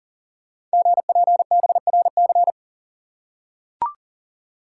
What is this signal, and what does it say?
Morse code, heard as a keyed tone, about 38 wpm, 690 Hz. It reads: GPBRC
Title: Code Groups - 5 Letters - 40wpm